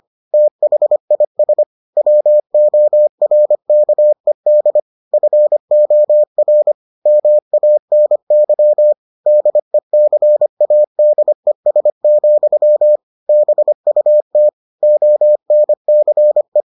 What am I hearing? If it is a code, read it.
Morse code, heard as a keyed tone, about 25 wpm, 600 Hz. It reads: THIS WORKED FOR MANY DECADES, BUT ONCE